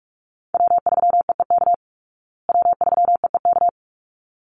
Morse code, heard as a keyed tone, about 44 wpm, 700 Hz. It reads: W3EEK W3EEK
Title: Call Signs Repeated 3x - 45wpm